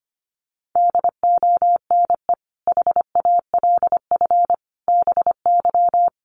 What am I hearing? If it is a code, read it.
Morse code, heard as a keyed tone, about 25 wpm, 700 Hz. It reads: DONE HALF BY
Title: Sentences from Top 800 Words 25wpm